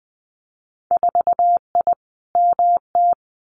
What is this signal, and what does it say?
Morse code, heard as a keyed tone, about 20 wpm, 700 Hz. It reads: 4I MT
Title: Winter Field Day Exchanges Encore 20wpm